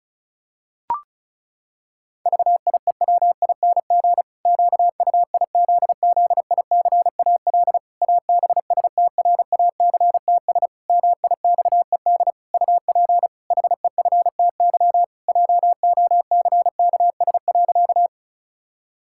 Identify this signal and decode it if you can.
Morse code, heard as a keyed tone, about 36 wpm, 700 Hz. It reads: VIEWING QUIZZICAL ABSTRACTS MIXED UP HEFTY JOCKS.
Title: Pangram Challenge v2 - 35wpm